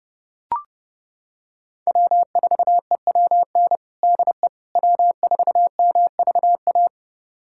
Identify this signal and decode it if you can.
Morse code, heard as a keyed tone, about 30 wpm, 700 Hz. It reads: W4EWN DE W4MVA
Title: USA Callsign DE Callsign - 5 Characters - 30wpm